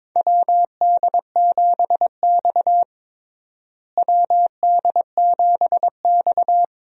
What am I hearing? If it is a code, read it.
Morse code, heard as a keyed tone, about 22 wpm, 700 Hz. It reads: WD7X WD7X
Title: USA Call Signs - 4 Characters - Repeated 2x - 22wpm